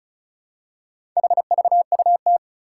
Morse code, heard as a keyed tone, about 36 wpm, 700 Hz. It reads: HVUT
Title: Four Challenging Letter-Number Combinations 35wpm